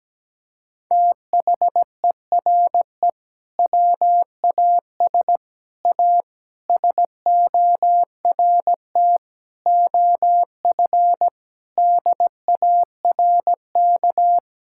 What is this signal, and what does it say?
Morse code, heard as a keyed tone, about 17 wpm, 700 Hz. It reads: THERE WAS A SORT OF DARK
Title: Sentences from Top 1600 Words 17wpm